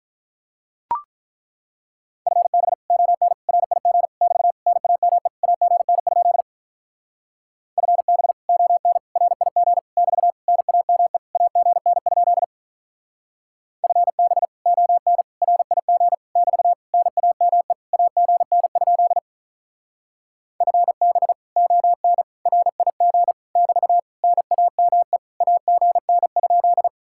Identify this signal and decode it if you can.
Morse code, heard as a keyed tone, about 42 wpm, 700 Hz. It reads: FB ON RIG <BT> NAME AGN? FB ON RIG <BT> NAME AGN? FB ON RIG <BT> NAME AGN? FB ON RIG <BT> NAME AGN?
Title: Speed-Racing Common QSO Phrases Encore - 35wpm